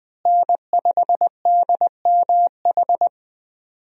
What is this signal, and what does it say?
Morse code, heard as a keyed tone, about 20 wpm, 700 Hz. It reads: N5DMH
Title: USA Call Signs - 5 Characters - Repeated 2x - 20wpm